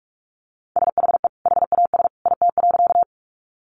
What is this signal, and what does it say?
Morse code, heard as a keyed tone, about 44 wpm, 700 Hz. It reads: SHE HAS IT.
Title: Sentences from Top 100 Words 45wpm